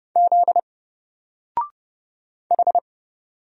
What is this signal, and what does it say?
Morse code, heard as a keyed tone, about 30 wpm, 700 Hz. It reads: Z H
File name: Single-Letter-Number-30wpm